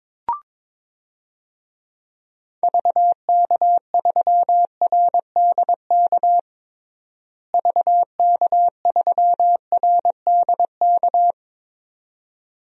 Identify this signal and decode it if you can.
Morse code, heard as a keyed tone, about 22 wpm, 700 Hz. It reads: VK3RDK VK3RDK
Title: DXCC-100 Call Signs - 6 Characters - Repeated 2x - 22wpm